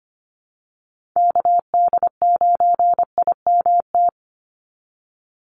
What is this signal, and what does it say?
Morse code, heard as a keyed tone, about 25 wpm, 700 Hz. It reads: KD9IMT
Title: USA Call Signs 25wpm